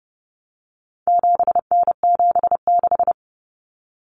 Morse code, heard as a keyed tone, about 30 wpm, 700 Hz. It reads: 7N76
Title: Common Cut Numbers - 4-Digit 30wpm